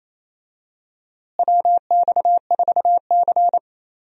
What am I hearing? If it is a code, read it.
Morse code, heard as a keyed tone, about 28 wpm, 700 Hz. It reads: WX4C
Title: USA Call Signs - 4 Characters - Repeated 2x - 28wpm